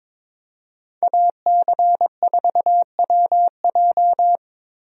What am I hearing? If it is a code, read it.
Morse code, heard as a keyed tone, about 22 wpm, 700 Hz. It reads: AC4WJ
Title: USA Call Signs - 5 Characters - Repeated 2x - 22wpm